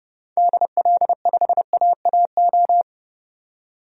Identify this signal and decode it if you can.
Morse code, heard as a keyed tone, about 30 wpm, 700 Hz. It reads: DL5AAO
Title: DXCC-100 Call Signs - 6 Characters - 30wpm